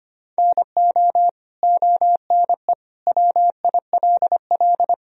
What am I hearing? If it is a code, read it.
Morse code, heard as a keyed tone, about 25 wpm, 700 Hz. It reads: NO ONE WILL
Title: Sentences from Top 100 Words and Abbreviations 25wpm